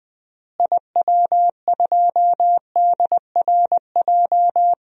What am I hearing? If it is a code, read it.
Morse code, heard as a keyed tone, about 20 wpm, 700 Hz. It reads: IW2DRJ
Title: DXCC-100 Call Signs - 6 Characters - Repeated 3x - 20wpm